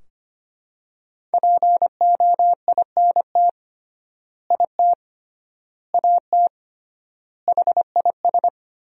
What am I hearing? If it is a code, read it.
Morse code, heard as a keyed tone, about 25 wpm, 700 Hz. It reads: POINT IT AT HIS
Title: 3x Word-Spacing - Sentences from Top 300 Words - 25wpm